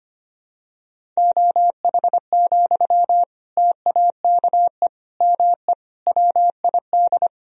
Morse code, heard as a keyed tone, about 25 wpm, 700 Hz. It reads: OH, TAKE ME WID
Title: Sentences from Top 500 Words and Abbreviations 25wpm